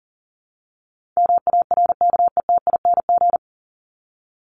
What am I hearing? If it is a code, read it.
Morse code, heard as a keyed tone, about 40 wpm, 700 Hz. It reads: MARKETING
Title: Top 751-1000 Words 40wpm